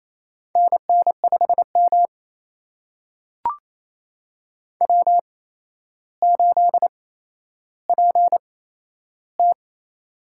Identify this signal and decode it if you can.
Morse code, heard as a keyed tone, about 28 wpm, spaced as for 12 wpm, 700 Hz. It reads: NN5M W8PT
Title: USA Call Signs - 4 Characters - 8x Character Spacing - 28wpm